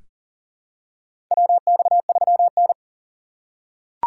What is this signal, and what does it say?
Morse code, heard as a keyed tone, about 40 wpm, 700 Hz. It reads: WX3N
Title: USA Call Signs - 4x Character Spacing - 40wpm